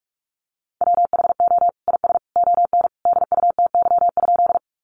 Morse code, heard as a keyed tone, about 44 wpm, 700 Hz. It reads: WHO IS ON DUTY?
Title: Sentences from Top 1500 Words 45wpm